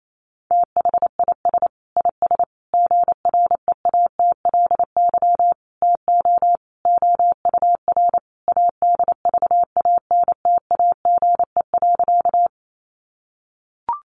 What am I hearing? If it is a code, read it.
Morse code, heard as a keyed tone, about 28 wpm, 700 Hz. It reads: THIS IS GREATLY TO OUR ADVANTAGE.
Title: Sentences from Top 1400 Words 28wpm